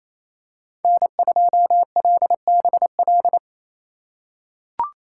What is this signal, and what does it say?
Morse code, heard as a keyed tone, about 28 wpm, 690 Hz. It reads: N2LBL
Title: USA Call Signs - 2x Character Spacing - 28wpm